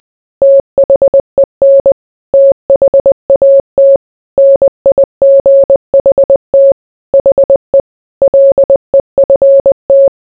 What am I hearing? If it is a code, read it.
Morse code, heard as a keyed tone, about 20 wpm, 550 Hz. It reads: THEN THAT NIGHT HE LEFT